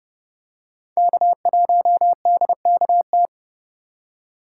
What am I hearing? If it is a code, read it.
Morse code, heard as a keyed tone, about 31 wpm, 700 Hz. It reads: K1DKT